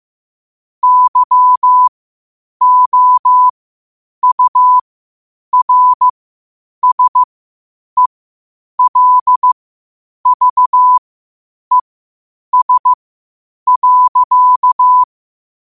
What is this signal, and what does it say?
Morse code, heard as a keyed tone, about 15 wpm, spaced as for 8 wpm, 1000 Hz. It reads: YOURSELVES.
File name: qotdfiles / 8wpm